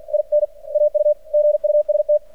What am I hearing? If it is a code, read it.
Morse code, heard as a keyed tone, about 50 wpm, spaced as for 42 wpm, 600 Hz. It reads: IN AUGUST
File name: archived / CRSnet